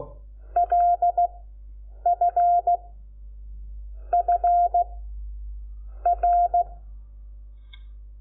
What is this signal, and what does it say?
Morse code, heard as a keyed tone, about 16 wpm, 700 Hz. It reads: L F F R
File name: Lecciones / CLASE Nº5